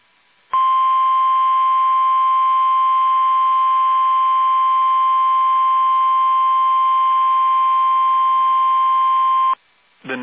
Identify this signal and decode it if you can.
Emergency Alert System (SAME) tones (the 1050 Hz attention tone). It is heard from 0.5 to 9.6 s.
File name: eas / KUSO / KUSO